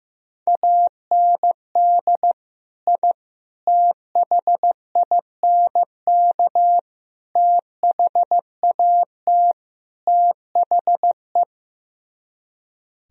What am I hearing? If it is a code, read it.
Morse code, heard as a keyed tone, about 15 wpm, 700 Hz. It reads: AND I THINK THAT THE
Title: Speed-Racing - Sets of 5 Words 15wpm